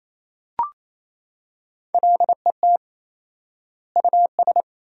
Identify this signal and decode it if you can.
Morse code, heard as a keyed tone, about 28 wpm, 700 Hz. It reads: LET US
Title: 4x Word-Spacing - Sentences from Top 300 Words - 28wpm